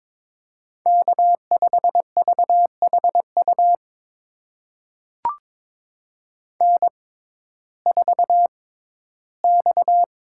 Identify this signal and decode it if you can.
Morse code, heard as a keyed tone, about 22 wpm, spaced as for 14 wpm, 700 Hz. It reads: K5VHU N4X
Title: USA Call Signs - 5 Characters - 6x Character Spacing - 22wpm